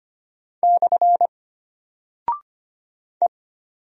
Morse code, heard as a keyed tone, about 25 wpm, 700 Hz. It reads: / E
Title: Single Character (Letter:Number:Punctuation:Prosign) Alt 25wpm